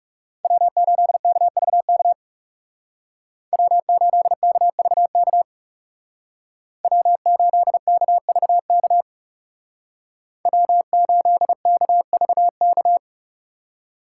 Morse code, heard as a keyed tone, about 39 wpm, 700 Hz. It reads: W8KVK W8KVK W8KVK W8KVK